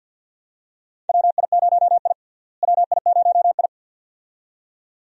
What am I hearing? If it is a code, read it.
Morse code, heard as a keyed tone, about 46 wpm, 700 Hz. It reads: WI0I WI0I
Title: USA Call Signs - 4 Characters - Repeated 2x - 50wpm